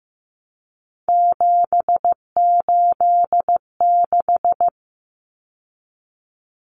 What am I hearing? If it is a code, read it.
Morse code, heard as a keyed tone, about 15 wpm, 700 Hz. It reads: 786
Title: Common Cut Numbers - 3-Digit 15wpm